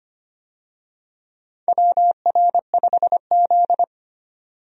Morse code, heard as a keyed tone, about 25 wpm, 700 Hz. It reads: WR5Z
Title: USA Call Signs - 4 Characters - 25wpm